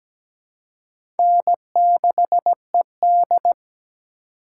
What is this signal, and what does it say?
Morse code, heard as a keyed tone, about 17 wpm, 700 Hz. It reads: N6ED
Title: USA Call Signs - 4 Characters - 2x Character Spacing - 17wpm